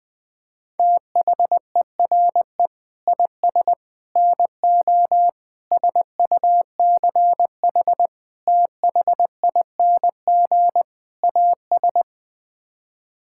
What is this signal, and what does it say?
Morse code, heard as a keyed tone, about 21 wpm, 700 Hz. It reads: THERE IS NO SUCH THING AS
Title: Sets of 6 Words - 20wpm